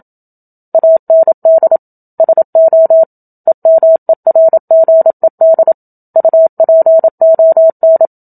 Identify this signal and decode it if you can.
Morse code, heard as a keyed tone, about 27 wpm, 650 Hz. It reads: AND SO EMERGED UPON
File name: Book Two / 01-Chapter 1 - 27 WPM